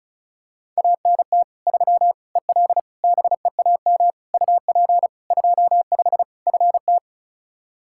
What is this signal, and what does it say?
Morse code, heard as a keyed tone, about 36 wpm, 700 Hz. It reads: ANT 3 EL BEAM UP 25 FT
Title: Practice QSO - Exchange 2 - Info - 35wpm